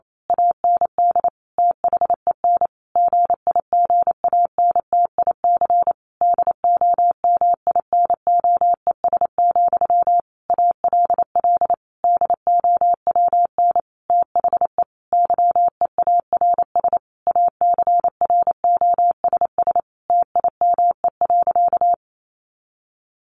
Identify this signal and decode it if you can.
Morse code, heard as a keyed tone, about 28 wpm, 700 Hz. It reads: AND THEN GIGANTIC DOMINOES, ALL DOWN THE YEARS ACROSS TIME.
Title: A Sound Of Thunder 28wpm